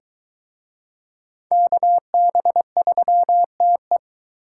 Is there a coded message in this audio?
Yes: Morse code from 1.5 to 4.0 s, heard as a keyed tone, about 22 wpm, 700 Hz. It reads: KB3TE